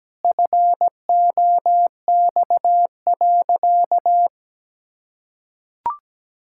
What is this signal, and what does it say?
Morse code, heard as a keyed tone, about 17 wpm, 700 Hz. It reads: FOX.
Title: Pangram Challenge v2 - 17wpm